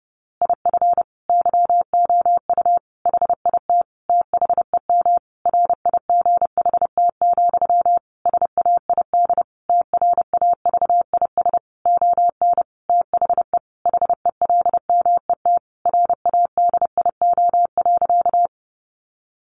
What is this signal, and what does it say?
Morse code, heard as a keyed tone, about 30 wpm, 700 Hz. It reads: IF YOU HIT THEM RIGHT, SAID TRAVIS ON THE HELMET RADIO.
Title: A Sound Of Thunder 30wpm